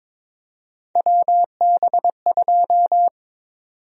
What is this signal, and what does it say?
Morse code, heard as a keyed tone, about 21 wpm, 700 Hz. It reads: WB2
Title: USA Call Sign Prefixes - Repeated 2x - 22wpm